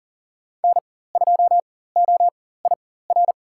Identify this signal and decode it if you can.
Morse code, heard as a keyed tone, about 40 wpm, spaced as for 19 wpm, 700 Hz. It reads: N2OIR
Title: USA Call Signs - 5 Characters - 4x Character Spacing - 40wpm